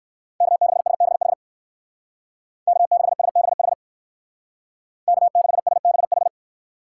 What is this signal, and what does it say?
Morse code, heard as a keyed tone, about 60 wpm, 700 Hz. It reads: K6SBH K6SBH K6SBH